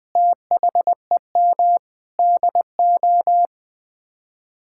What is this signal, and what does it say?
Morse code, heard as a keyed tone, about 20 wpm, 700 Hz. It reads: THEM DO
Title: Sets of 9 Words - 20wpm